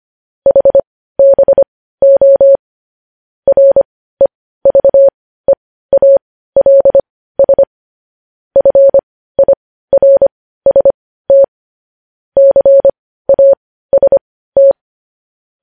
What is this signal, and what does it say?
Morse code, heard as a keyed tone, about 25 wpm, spaced as for 15 wpm, 550 Hz. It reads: HBO REVEALS FIRST CAST
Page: news headlines in morse code at 15 wpm